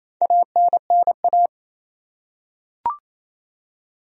Morse code, heard as a keyed tone, about 28 wpm, 700 Hz. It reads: ANNA
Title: USA Common Names - Max 4 Letters - 28wpm